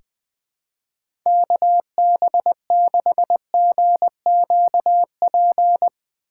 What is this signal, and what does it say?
Morse code, heard as a keyed tone, about 20 wpm, 700 Hz. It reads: KB6GQP
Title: USA Call Signs - 6 Characters - 3x Character Spacing - 20wpm